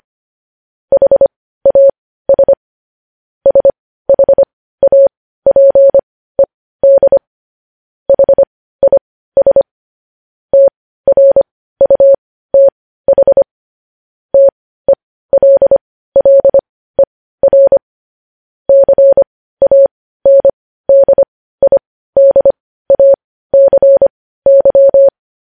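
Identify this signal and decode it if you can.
Morse code, heard as a keyed tone, about 25 wpm, spaced as for 15 wpm, 550 Hz. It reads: HAS SHAPED HIS TRUTH TELLER CANDIDACY
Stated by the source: Morse practice sending news headlines at 15 wpm.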